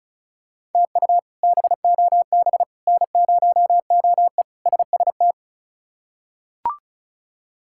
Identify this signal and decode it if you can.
Morse code, heard as a keyed tone, about 34 wpm, 700 Hz. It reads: TU BOB N0OE SST
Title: K1USN SST QSO Practice - 35wpm